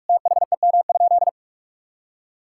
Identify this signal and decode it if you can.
Morse code, heard as a keyed tone, about 45 wpm, 700 Hz. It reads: THEM?
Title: Sentences from Top 1000 Words - Review Part 5 - 45wpm